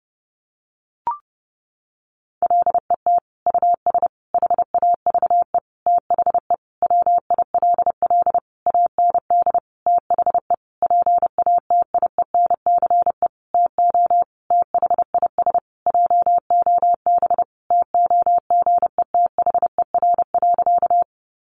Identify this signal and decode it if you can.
Morse code, heard as a keyed tone, about 28 wpm, 700 Hz. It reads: LET US HAVE THE WILL AND THE PATIENCE TO THIS JOB TOGETHER.
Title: Sentences from Top 1300 Words 30wpm